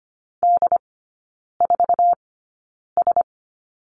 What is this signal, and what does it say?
Morse code, heard as a keyed tone, about 25 wpm, spaced as for 9 wpm, 700 Hz. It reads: D4S
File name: course / 25:10wpm / 093 Lesson 16e - Letter c - Three Characters 25wpm